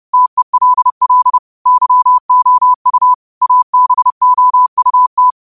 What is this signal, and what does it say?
Morse code, heard as a keyed tone, about 30 wpm, 1000 Hz. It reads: TELL YOU ABOUT
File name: qotdfiles / 30wpm